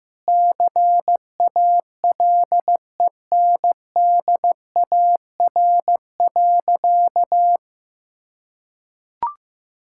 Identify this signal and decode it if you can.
Morse code, heard as a keyed tone, about 15 wpm, 700 Hz. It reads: CALENDAR.
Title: Sentences from Top 1600 Words 15wpm